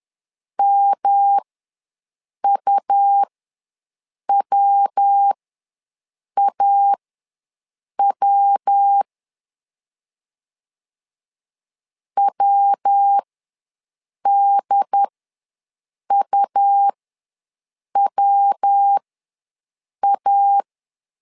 Morse code, heard as a keyed tone, about 10 wpm, spaced as for 6 wpm, 800 Hz. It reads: MUWAW WDUWA
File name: LEZIONI / 3 MWRO / 3 MW ws 10wpm 2213